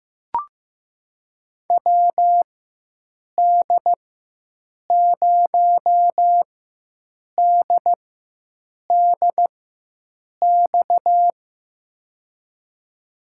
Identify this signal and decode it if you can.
Morse code, heard as a keyed tone, about 15 wpm, spaced as for 7 wpm, 700 Hz. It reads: WD0DDX
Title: USA Call Signs - 6 Characters - 4x Character Spacing - 15wpm